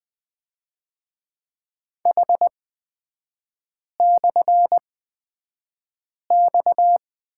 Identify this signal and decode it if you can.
Morse code, heard as a keyed tone, about 20 wpm, spaced as for 5 wpm, 690 Hz. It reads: H/X
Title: 229 Lesson 37e - Letter x - Three Characters 20wpm